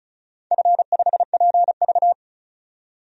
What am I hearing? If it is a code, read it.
Morse code, heard as a keyed tone, about 36 wpm, 700 Hz. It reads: F5PV
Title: DXCC-100 Call Signs - 4 Characters - 35wpm